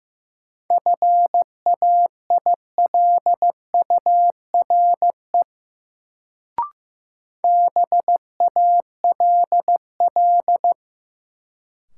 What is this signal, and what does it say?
Morse code, heard as a keyed tone, about 15 wpm, 700 Hz. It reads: FAILURE BALL
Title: Sentences from Top 900 Words - New Word Drill - 15wpm